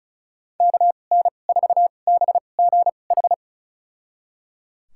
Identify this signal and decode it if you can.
Morse code, heard as a keyed tone, about 34 wpm, spaced as for 25 wpm, 700 Hz. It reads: KN4BGH
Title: USA Call Signs - 6 Characters - 2x Character Spacing - 35wpm